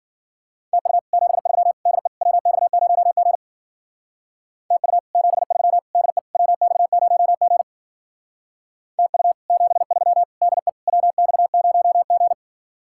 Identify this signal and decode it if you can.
Morse code, heard as a keyed tone, about 47 wpm, 700 Hz. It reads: TU 73 DE WX0G TU 73 DE WX0G TU 73 DE WX0G
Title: Speed-Racing Common QSO Phrases Encore - 40wpm